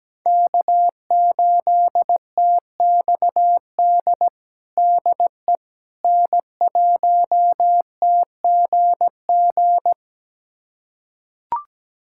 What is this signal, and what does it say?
Morse code, heard as a keyed tone, about 17 wpm, 700 Hz. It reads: K8TXD DE N1TGG
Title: USA Callsign DE Callsign - 5 Characters - 17wpm